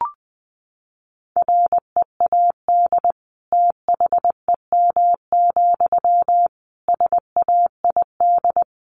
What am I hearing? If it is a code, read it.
Morse code, heard as a keyed tone, about 20 wpm, 700 Hz. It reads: READ THEM, SAID
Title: Sentences from Top 400 Words 20wpm